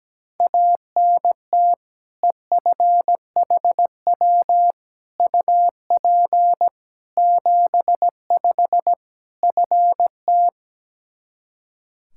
Morse code, heard as a keyed tone, about 17 wpm, 700 Hz. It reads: ANT EFHW UP 75 FT
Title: K1USN SST QSO Practice 17wpm